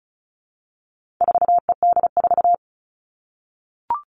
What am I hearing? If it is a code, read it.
Morse code, heard as a keyed tone, about 34 wpm, 700 Hz. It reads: 4ED4
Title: All Cut Numbers - 4-Digit 35wpm